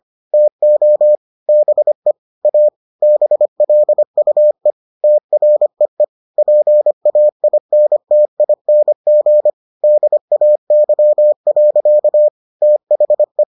Morse code, heard as a keyed tone, about 25 wpm, 600 Hz. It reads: TO BE A BLUE TREE PAINTING DAY. THE